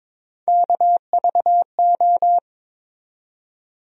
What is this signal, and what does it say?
Morse code, heard as a keyed tone, about 22 wpm, 700 Hz. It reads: KVO
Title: USA Call Sign Postfixes - 3 Letters 22wpm